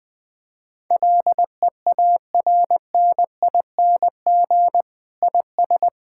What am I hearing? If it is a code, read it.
Morse code, heard as a keyed tone, about 20 wpm, 700 Hz. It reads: LEARNING IS
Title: Difficult Sentences with Hints - 20wpm